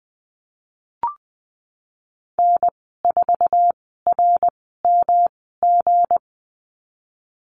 Morse code, heard as a keyed tone, about 20 wpm, spaced as for 15 wpm, 700 Hz. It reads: N4RMG
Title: 173 Lesson 28g - Number 9 - Call Signs 20wpm